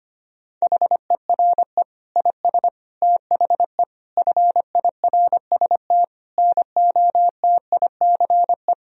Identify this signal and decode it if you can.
Morse code, heard as a keyed tone, about 25 wpm, 700 Hz. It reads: HERE IS THE FIRST NOTICE
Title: Sentences from Top 1000 Words - Review Part 2 - 25wpm